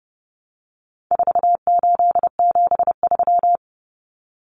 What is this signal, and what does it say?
Morse code, heard as a keyed tone, about 28 wpm, 700 Hz. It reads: 4873
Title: Common Cut Numbers - 4-Digit 30wpm